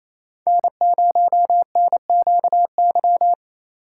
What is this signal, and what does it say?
Morse code, heard as a keyed tone, about 29 wpm, 700 Hz. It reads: N0NQY
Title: USA Call Signs - 3x Character Spacing - 28wpm